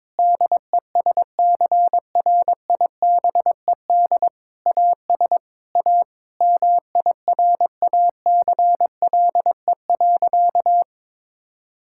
Morse code, heard as a keyed tone, about 22 wpm, 700 Hz. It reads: DESCRIBED AS A MIRACLE.
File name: Sentences with 7-Word Phrases - 22wpm